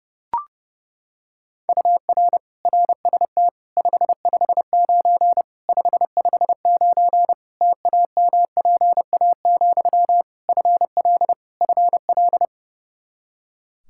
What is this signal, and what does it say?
Morse code, heard as a keyed tone, about 32 wpm, 700 Hz. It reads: UR RST 559 559 TAMPA, FL FL
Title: Practice QSO - Exchange 1 - Beginning - 30wpm